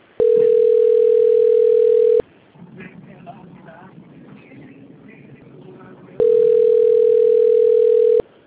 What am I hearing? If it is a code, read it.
Ringback tone (North America), 440 + 480 Hz, with speech.
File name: rg-1600-unknown- - - .42376